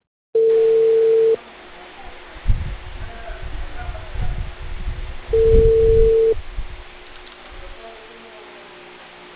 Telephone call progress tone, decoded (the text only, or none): ringback tone